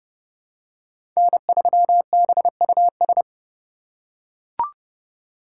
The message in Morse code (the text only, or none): N3BUS